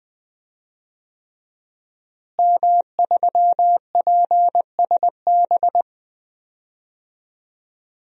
Morse code, heard as a keyed tone, about 20 wpm, 700 Hz. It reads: M3PSB
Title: DXCC-100 Call Signs - 5 Characters - Repeated 3x - 20wpm